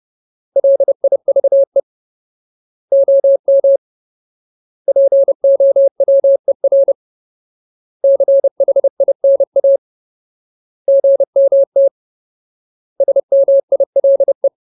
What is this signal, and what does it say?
Morse code, heard as a keyed tone, about 30 wpm, 550 Hz. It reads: LIVE OM POWER CHINA GMT SMILE
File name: Ultimate Word List / 30WPM Ultimate